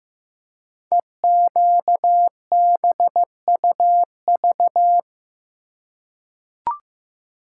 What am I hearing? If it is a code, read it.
Morse code, heard as a keyed tone, about 15 wpm, 700 Hz. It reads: EQBUV
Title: Code Groups - 5 Letters - 15wpm